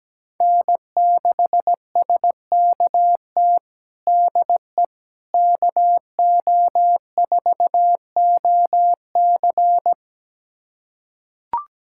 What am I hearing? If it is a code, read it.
Morse code, heard as a keyed tone, about 17 wpm, 700 Hz. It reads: N6SKT DE KO4OC
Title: USA Callsign DE Callsign - 5 Characters - 17wpm